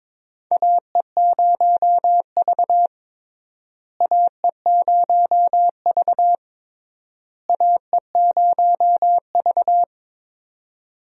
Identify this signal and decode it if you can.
Morse code, heard as a keyed tone, about 22 wpm, 700 Hz. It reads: AE0V AE0V AE0V
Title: USA Call Signs - 4 Characters - Repeated 3x - 22wpm